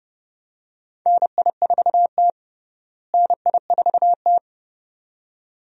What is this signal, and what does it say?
Morse code, heard as a keyed tone, about 29 wpm, 700 Hz. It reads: NI4T NI4T
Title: USA Call Signs - 4 Characters - Repeated 2x - 30wpm